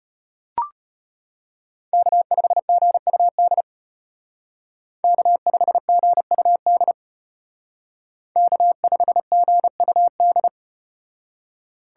Morse code, heard as a keyed tone, about 34 wpm, 700 Hz. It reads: K5GUD K5GUD K5GUD